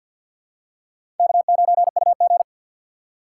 Morse code, heard as a keyed tone, about 50 wpm, 700 Hz. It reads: K9UG
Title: USA Call Signs - 4 Characters - 6x Character Spacing - 50wpm